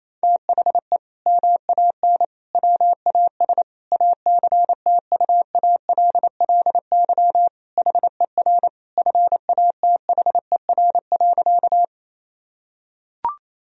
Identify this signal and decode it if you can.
Morse code, heard as a keyed tone, about 27 wpm, 700 Hz. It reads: THE MAN WAS ACTUALLY HER FATHER.
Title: Sentences from Top 1000 Words - Review Part 4 - 28wpm